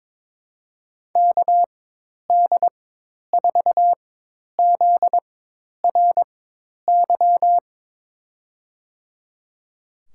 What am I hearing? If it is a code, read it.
Morse code, heard as a keyed tone, about 22 wpm, spaced as for 10 wpm, 700 Hz. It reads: KD4ZRY
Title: USA Call Signs - 4x Character Spacing - 22wpm